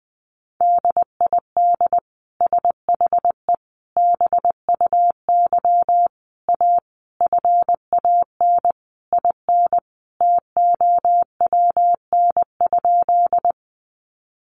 Morse code, heard as a keyed tone, about 20 wpm, 700 Hz. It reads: DID SHE BUY A FAN IN TOWN?
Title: Sentences from Top 1300 Words 20wpm